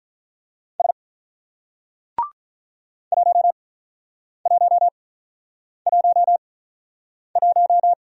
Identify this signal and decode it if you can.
Morse code, heard as a keyed tone, about 49 wpm, spaced as for 9 wpm, 700 Hz. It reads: S 1111